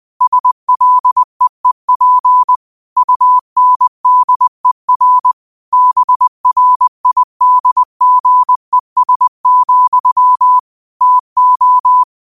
Morse code, heard as a keyed tone, about 20 wpm, 1000 Hz. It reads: SLEEP UNDER BRIDGES, TO